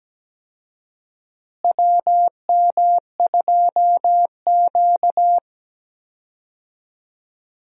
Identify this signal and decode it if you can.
Morse code, heard as a keyed tone, about 17 wpm, 700 Hz. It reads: WM2Q
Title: USA Call Signs - 4 Characters - Repeated 2x - 17wpm